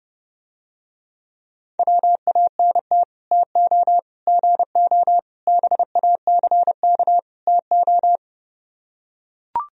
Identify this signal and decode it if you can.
Morse code, heard as a keyed tone, about 30 wpm, 700 Hz. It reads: WANT TO GO BACK TO